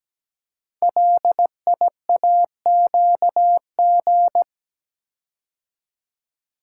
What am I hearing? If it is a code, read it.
Morse code, heard as a keyed tone, about 17 wpm, 700 Hz. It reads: LIAQG
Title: Code Groups - 5 Letters - 17wpm